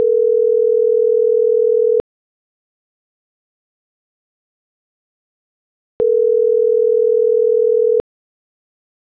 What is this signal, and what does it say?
Ringback tone (North America), 440 + 480 Hz.
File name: CENTRAL NMP / out patricia.martineznmp